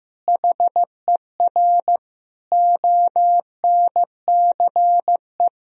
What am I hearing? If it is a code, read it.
Morse code, heard as a keyed tone, about 15 wpm, 700 Hz. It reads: HER ONCE
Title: Sentences from Top 500 Words and Abbreviations 15wpm